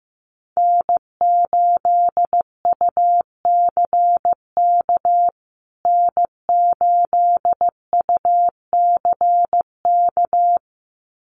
Morse code, heard as a keyed tone, about 15 wpm, 700 Hz. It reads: N8UCK N8UCK
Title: USA Call Signs Repeated 2x 15wpm